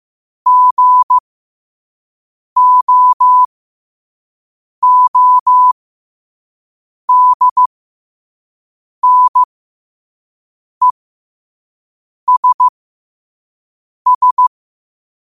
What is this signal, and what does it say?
Morse code, heard as a keyed tone, about 15 wpm, spaced as for 5 wpm, 1000 Hz. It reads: GOODNESS
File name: qotdfiles / 5wpm